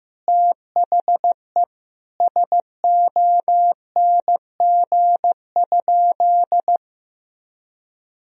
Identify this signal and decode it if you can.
Morse code, heard as a keyed tone, about 15 wpm, 700 Hz. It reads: THE SONG?
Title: Sentences from Top 1000 Words - Review Part 2 - 15wpm